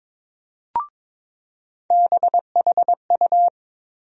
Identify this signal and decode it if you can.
Morse code, heard as a keyed tone, about 22 wpm, 700 Hz. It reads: BHU
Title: USA Call Sign Postfixes - 3 Letters 22wpm